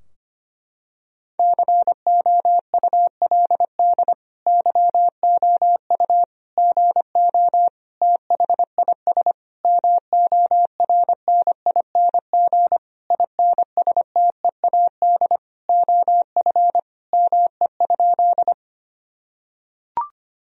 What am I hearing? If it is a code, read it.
Morse code, heard as a keyed tone, about 25 wpm, 700 Hz. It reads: COULD YOU GO THIS MORNING INSTEAD OF ME?
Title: Sentences from Top 1000 Words - Review Part 4 - 25wpm